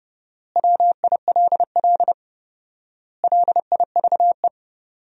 Morse code, heard as a keyed tone, about 30 wpm, 700 Hz. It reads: WILL LIVE
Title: 4x Word-Spacing - Sentences from Top 300 Words - 30wpm